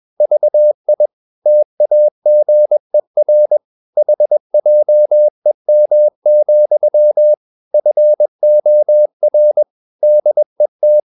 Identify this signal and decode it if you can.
Morse code, heard as a keyed tone, about 21 wpm, 600 Hz. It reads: VI TAGER HJEM, FOR DET
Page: Ferie 20 wpm